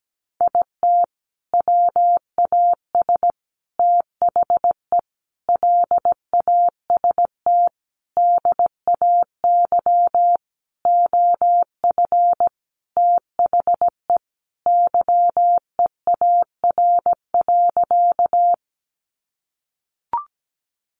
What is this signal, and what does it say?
Morse code, heard as a keyed tone, about 17 wpm, 700 Hz. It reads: IT WAS THE LAST DAY OF THE YEAR.
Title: Sentences from Top 300 Words 17wpm